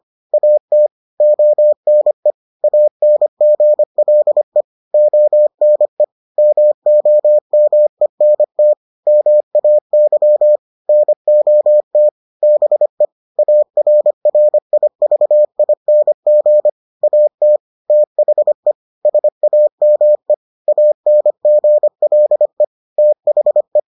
Morse code, heard as a keyed tone, about 25 wpm, 600 Hz. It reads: AT ONE ANGLE ONE MOMENT MAY NOT BE ARRIVING AT THE SAME ANGLE THE